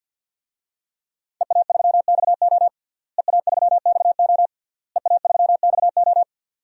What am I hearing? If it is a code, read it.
Morse code, heard as a keyed tone, about 52 wpm, 700 Hz. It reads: EA3XO EA3XO EA3XO
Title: DXCC-100 Call Signs - 5 Characters - Repeated 3x - 50wpm